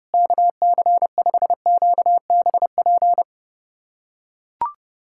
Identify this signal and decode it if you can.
Morse code, heard as a keyed tone, about 30 wpm, 700 Hz. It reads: KC5QBP
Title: USA Callsign DE Callsign - 6 Characters - 30wpm